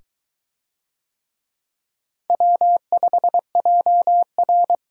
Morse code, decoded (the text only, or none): W5JR